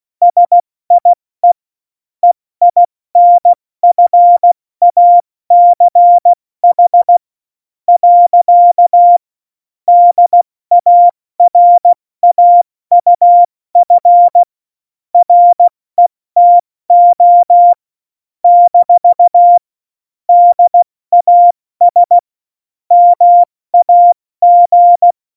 Morse code, heard as a keyed tone, about 16 wpm, spaced as for 14 wpm, 700 Hz. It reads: SIE EINFACH. DARAUF RETO - DAS MAG
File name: Morsekurs / Witze / Witz 19. Dezember 2017, 70 F80